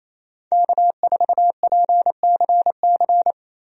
Morse code, heard as a keyed tone, about 27 wpm, 700 Hz. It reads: K4PCC
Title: USA Call Signs - 5 Characters - Repeated 3x - 28wpm